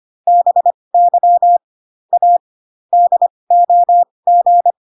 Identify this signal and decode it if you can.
Morse code, heard as a keyed tone, about 25 wpm, spaced as for 20 wpm, 700 Hz. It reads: BY A DOG